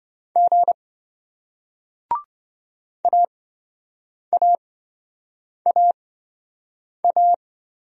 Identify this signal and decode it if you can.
Morse code, heard as a keyed tone, about 27 wpm, spaced as for 7 wpm, 700 Hz. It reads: G AAAA